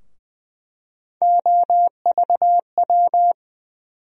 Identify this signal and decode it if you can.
Morse code, heard as a keyed tone, about 20 wpm, 700 Hz. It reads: OVW
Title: USA Call Sign Postfixes - 3 Letters 20wpm